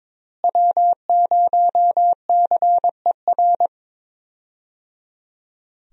Morse code, heard as a keyed tone, about 22 wpm, 700 Hz. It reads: W0CER
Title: USA Call Signs - 5 Characters - Repeated 3x - 22wpm